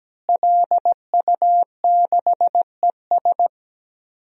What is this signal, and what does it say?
Morse code, heard as a keyed tone, about 17 wpm, 700 Hz. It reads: LU6ES
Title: DXCC-100 Call Signs - 5 Characters - Repeated 2x - 17wpm